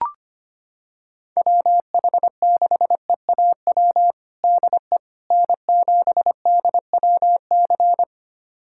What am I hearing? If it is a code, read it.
Morse code, heard as a keyed tone, about 25 wpm, 700 Hz. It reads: WH6EAW DE N7DWC
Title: USA Callsign DE Callsign 25wpm